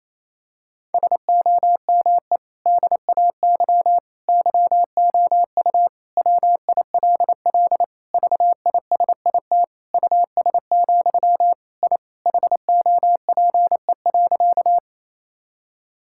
Morse code, heard as a keyed tone, about 27 wpm, 700 Hz. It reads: SOME DAY YOU WILL VISIT US, I HOPE.
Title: Sentences from Top 1000 Words - Review Part 6 - 28wpm